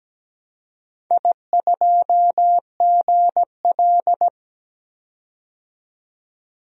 Morse code, heard as a keyed tone, about 17 wpm, 700 Hz. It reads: I2GL
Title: DXCC-100 Call Signs - 4 Characters - 17wpm